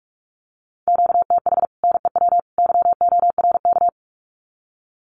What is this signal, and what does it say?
Morse code, heard as a keyed tone, about 44 wpm, 700 Hz. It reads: QTH NEW YORK
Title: Common QSO Phrases 45wpm